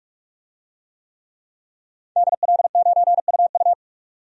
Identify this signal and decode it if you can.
Morse code, heard as a keyed tone, about 44 wpm, 700 Hz. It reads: DL9UU
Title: DXCC-100 Call Signs - 5 Characters - 45wpm